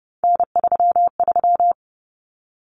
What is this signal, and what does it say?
Morse code, heard as a keyed tone, about 30 wpm, 700 Hz. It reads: N33
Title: Common Cut Numbers - 3-Digit 30wpm